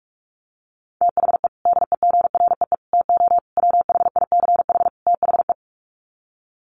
Morse code, heard as a keyed tone, about 44 wpm, 700 Hz. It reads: THE DEGREE TO WHICH THE
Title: Sets of 5 Words Encore2 45wpm